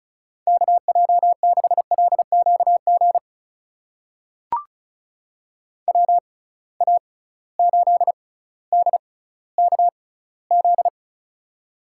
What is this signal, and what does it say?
Morse code, heard as a keyed tone, about 34 wpm, spaced as for 18 wpm, 700 Hz. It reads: KJ6LQG WA8DKZ
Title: USA Call Signs - 6 Characters - 6x Character Spacing - 35wpm